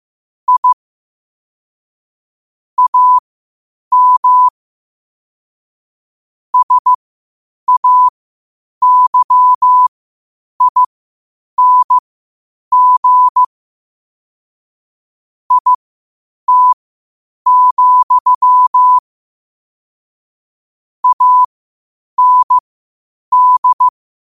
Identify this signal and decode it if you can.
Morse code, heard as a keyed tone, about 15 wpm, spaced as for 8 wpm, 1000 Hz. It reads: I AM SAYING IT, AND
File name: qotdfiles / 8wpm